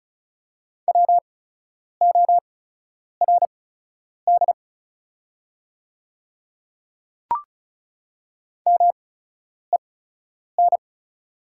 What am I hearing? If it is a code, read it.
Morse code, heard as a keyed tone, about 35 wpm, spaced as for 10 wpm, 700 Hz. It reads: WORD MEN
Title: 8x Character Spacing - Top 300 Words - 35wpm